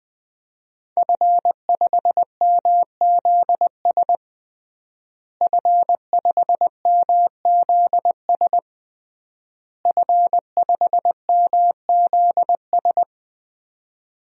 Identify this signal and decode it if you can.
Morse code, heard as a keyed tone, about 21 wpm, 700 Hz. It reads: F5MZS F5MZS F5MZS
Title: DXCC-100 Call Signs - 5 Characters - Repeated 3x - 20wpm